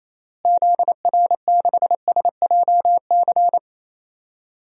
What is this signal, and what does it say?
Morse code, heard as a keyed tone, about 28 wpm, 700 Hz. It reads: ZR6SJC
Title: DXCC-100 Call Signs - 6 Characters - Repeated 3x - 28wpm